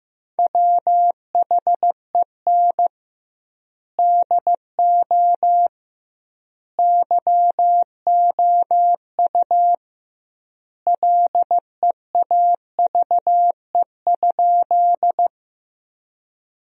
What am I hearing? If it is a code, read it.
Morse code, heard as a keyed tone, about 15 wpm, 700 Hz. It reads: WHEN DO YOU LEAVE?
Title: 2x Word-Spacing - Sentences from Top 400 Words - 15wpm